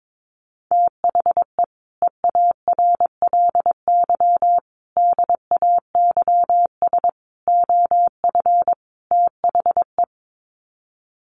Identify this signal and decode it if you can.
Morse code, heard as a keyed tone, about 22 wpm, 700 Hz. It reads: THE EARLY DAYS OF THE
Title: Sets of 5 Words Encore 22wpm